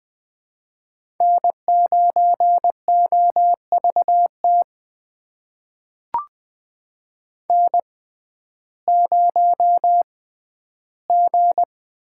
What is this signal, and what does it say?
Morse code, heard as a keyed tone, about 20 wpm, spaced as for 12 wpm, 700 Hz. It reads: N9OVT N0G
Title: USA Call Signs - 5 Characters - 6x Character Spacing - 20wpm